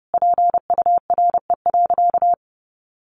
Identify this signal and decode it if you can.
Morse code, heard as a keyed tone, about 30 wpm, 700 Hz. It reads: PURE.
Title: Sentences from Top 1500 Words 30wpm